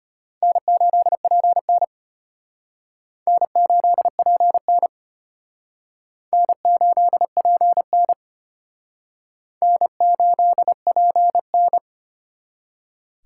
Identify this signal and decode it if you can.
Morse code, heard as a keyed tone, about 34 wpm, 700 Hz. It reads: N8PN N8PN N8PN N8PN